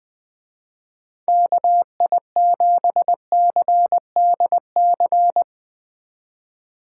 Morse code, heard as a keyed tone, about 20 wpm, 700 Hz. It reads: KI7CDC
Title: Speed Racing - USA 2x3 Call Signs 20wpm